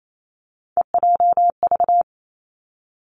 Morse code, heard as a keyed tone, about 28 wpm, 700 Hz. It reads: EJV
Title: Three Challenging Letter-Number Combinations 28wpm